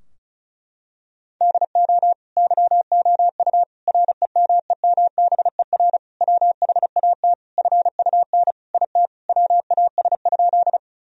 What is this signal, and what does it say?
Morse code, heard as a keyed tone, about 36 wpm, 700 Hz. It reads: DO YOU REMEMBER WHAT FUN IT WAS?
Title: Sentences from Top 1000 Words - Review Part 4 - 35wpm